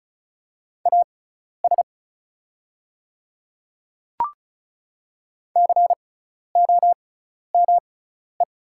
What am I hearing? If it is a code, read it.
Morse code, heard as a keyed tone, about 34 wpm, spaced as for 12 wpm, 700 Hz. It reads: AS COME
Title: 6x Character Spacing - Top 100 Words - 35wpm